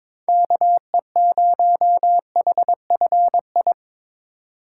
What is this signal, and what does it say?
Morse code, heard as a keyed tone, about 25 wpm, 700 Hz. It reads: KE0HFI
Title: Speed Racing - USA Call Signs 22wpm